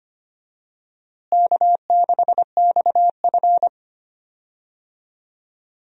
Morse code, heard as a keyed tone, about 25 wpm, 700 Hz. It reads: K6XF
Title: USA Call Signs - 4 Characters - Repeated 2x - 25wpm